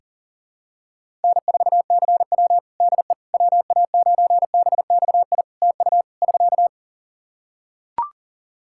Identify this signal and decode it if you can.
Morse code, heard as a keyed tone, about 37 wpm, 700 Hz. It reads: N4CW DE WA9BXI TU <SK>
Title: Practice QSO - Exchange 4 - Ending - 40wpm